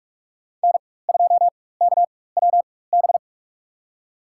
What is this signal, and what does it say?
Morse code, heard as a keyed tone, about 47 wpm, spaced as for 21 wpm, 700 Hz. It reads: N2KWB
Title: USA Call Signs - 5 Characters - 4x Character Spacing - 45wpm